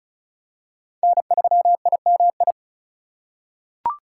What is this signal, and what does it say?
Morse code, heard as a keyed tone, about 34 wpm, 700 Hz. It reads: N3IMI
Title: USA Call Signs - 4x Character Spacing - 35wpm